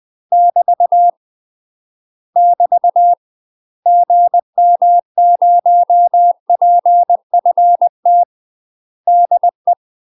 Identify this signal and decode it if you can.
Morse code, heard as a keyed tone, about 20 wpm, 700 Hz. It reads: <BT> <BT> GM0PFT DE
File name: UK MASTER QSOs TRAINING / 20WPM / UK MASTER QSOs 0066